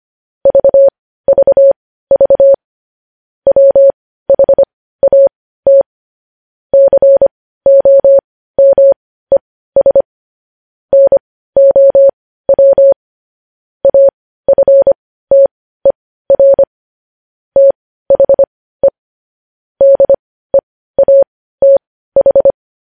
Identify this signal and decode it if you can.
Morse code, heard as a keyed tone, about 25 wpm, spaced as for 15 wpm, 550 Hz. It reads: VVV WHAT COMES NOW AFTER THE DEATH
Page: News Headlines in Morse Code at 15 WPM